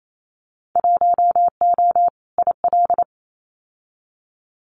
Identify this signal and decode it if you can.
Morse code, heard as a keyed tone, about 28 wpm, 700 Hz. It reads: 1O IL
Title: Winter Field Day Exchanges 28wpm